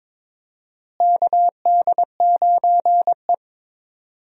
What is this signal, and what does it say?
Morse code, heard as a keyed tone, about 21 wpm, 700 Hz. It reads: KD9E
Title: USA Call Signs - 4 Characters - Repeated 3x - 22wpm